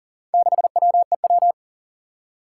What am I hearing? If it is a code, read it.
Morse code, heard as a keyed tone, about 40 wpm, 700 Hz. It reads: 6WEW
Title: Four Challenging Letter-Number Combinations 40wpm